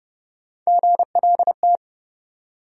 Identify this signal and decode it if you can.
Morse code, heard as a keyed tone, about 29 wpm, 700 Hz. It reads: GLT
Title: USA Call Sign Postfixes - 3 Letters 30wpm